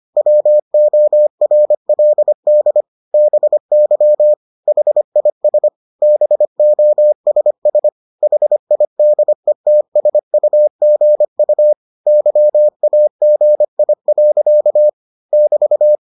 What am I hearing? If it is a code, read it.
Morse code, heard as a keyed tone, about 25 wpm, 600 Hz. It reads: WORLD BY HIS BOSS HIDETSUGU YAGI. <BT>